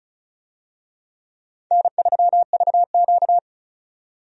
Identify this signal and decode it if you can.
Morse code, heard as a keyed tone, about 36 wpm, 700 Hz. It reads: N3VQ
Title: USA Call Signs - 4 Characters - 35wpm